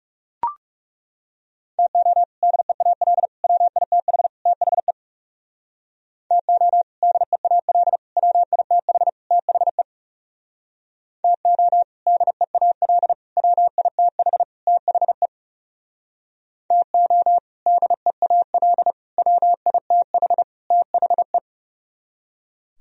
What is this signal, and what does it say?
Morse code, heard as a keyed tone, about 38 wpm, 700 Hz. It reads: TO DEAL WITH THE TO DEAL WITH THE TO DEAL WITH THE TO DEAL WITH THE